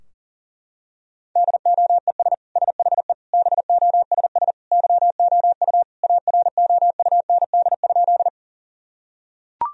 Morse code, heard as a keyed tone, about 40 wpm, 700 Hz. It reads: DOES SHE BOSS YOU AROUND?
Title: Sentences from Top 1000 Words - Review Part 4 - 40wpm